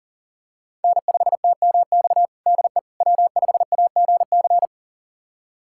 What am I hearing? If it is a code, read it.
Morse code, heard as a keyed tone, about 37 wpm, 700 Hz. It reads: N5TMX DE W5AGC
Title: USA Callsign DE Callsign - 5 Characters - 40wpm